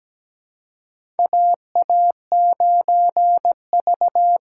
Morse code, heard as a keyed tone, about 17 wpm, 700 Hz. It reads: AA9V
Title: USA Call Signs - 4 Characters - 4x Character Spacing - 17wpm